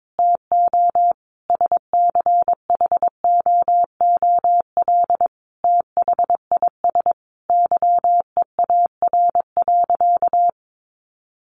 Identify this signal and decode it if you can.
Morse code, heard as a keyed tone, about 22 wpm, 700 Hz. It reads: TO SCHOOL THIS YEAR.
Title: Sentences from Top 1400 Words 22wpm